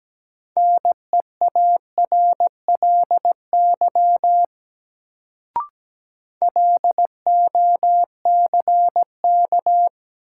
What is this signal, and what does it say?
Morse code, heard as a keyed tone, about 17 wpm, 700 Hz. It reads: NEARLY LOCK
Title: Sentences from Top 1300 Words - New Word Drill Alt - 17wpm